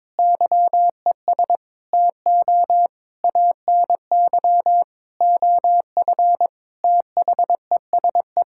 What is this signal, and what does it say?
Morse code, heard as a keyed tone, about 22 wpm, 700 Hz. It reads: YES TO ANY OF THESE
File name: Sets of 9 Words - 22wpm